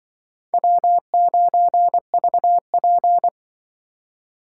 Morse code, heard as a keyed tone, about 23 wpm, 700 Hz. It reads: W9VP
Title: Speed Racing - USA 1x2 Call Signs 20wpm